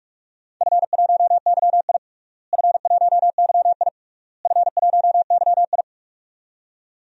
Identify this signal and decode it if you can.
Morse code, heard as a keyed tone, about 43 wpm, 700 Hz. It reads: F1YI F1YI F1YI
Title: DXCC-100 Call Signs - 4 Characters - Repeated 3x - 45wpm